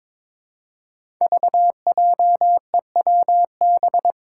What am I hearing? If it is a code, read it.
Morse code, heard as a keyed tone, about 21 wpm, 700 Hz. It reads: VJEWB
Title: Five Challenging Letter-Number Combinations 22wpm